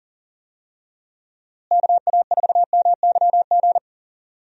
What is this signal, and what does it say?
Morse code, heard as a keyed tone, about 40 wpm, 700 Hz. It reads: KA4MYG
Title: USA Call Signs - 6 Characters - Repeated 3x - 40wpm